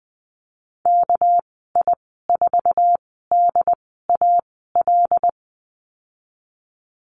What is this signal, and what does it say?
Morse code, heard as a keyed tone, about 20 wpm, spaced as for 15 wpm, 700 Hz. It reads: KI4DAL
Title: 207 Lesson 33g - Letter j - Call Signs 20wpm